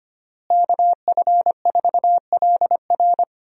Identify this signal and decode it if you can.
Morse code, heard as a keyed tone, about 25 wpm, 700 Hz. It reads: KF4LR
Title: USA Call Signs - 5 Characters - Repeated 3x - 25wpm